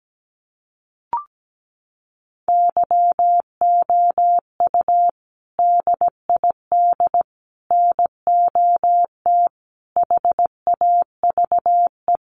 Morse code, heard as a keyed tone, about 17 wpm, 700 Hz. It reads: YOU DID NOT HAVE
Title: Sets of 5 Words Encore 17wpm